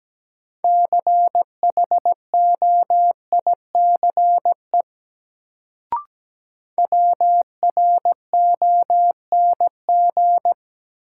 Morse code, heard as a keyed tone, about 17 wpm, 700 Hz. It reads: CHOICE WRONG
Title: Sentences from Top 900 Words - New Word Drill - 17wpm